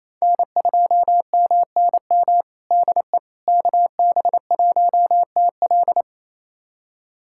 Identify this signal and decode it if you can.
Morse code, heard as a keyed tone, about 28 wpm, 700 Hz. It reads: N2MNM DE KB1TL
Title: USA Callsign DE Callsign - 5 Characters - 28wpm